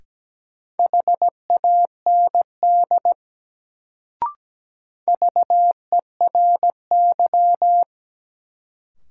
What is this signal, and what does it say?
Morse code, heard as a keyed tone, about 17 wpm, 700 Hz. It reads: HAND VERY
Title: Sentences from Top 200 Words - New Word Drill - 17wpm